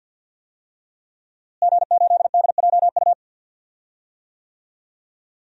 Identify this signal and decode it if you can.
Morse code, heard as a keyed tone, about 47 wpm, 700 Hz. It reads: G8DJU
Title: DXCC-100 Call Signs - 5 Characters - Repeated 3x - 50wpm